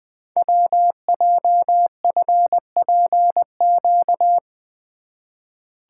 Morse code, heard as a keyed tone, about 20 wpm, 700 Hz. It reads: WJFPQ
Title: Code Groups - 5 Letters - 20wpm